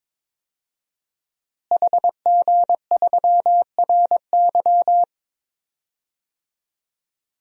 Morse code, heard as a keyed tone, about 22 wpm, 700 Hz. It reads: HG3RY
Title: DXCC-100 Call Signs - 5 Characters - Repeated 3x - 22wpm